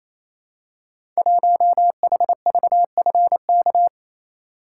Morse code, heard as a keyed tone, about 28 wpm, 700 Hz. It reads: 1HVFK
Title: Code Groups - 5 Letter:Numbers - 28wpm